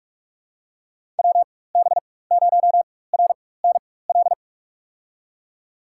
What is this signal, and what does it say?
Morse code, heard as a keyed tone, about 43 wpm, spaced as for 21 wpm, 700 Hz. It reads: WB0RNL
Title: USA Call Signs - 6 Characters - 4x Character Spacing - 45wpm